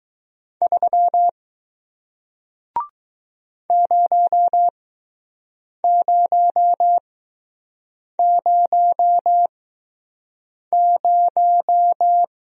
Morse code, heard as a keyed tone, about 20 wpm, spaced as for 6 wpm, 700 Hz. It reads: 3 0000